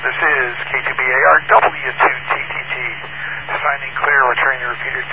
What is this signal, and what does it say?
Emergency Alert System (SAME) tones (header bursts), heard as harsh buzzing bursts, with speech. It is heard from 0.4 to 4.9 s.